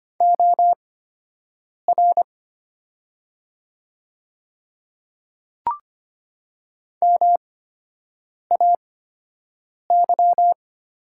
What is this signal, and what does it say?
Morse code, heard as a keyed tone, about 25 wpm, spaced as for 7 wpm, 700 Hz. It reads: OR MAY
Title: 8x Character Spacing - Top 100 Words - 25wpm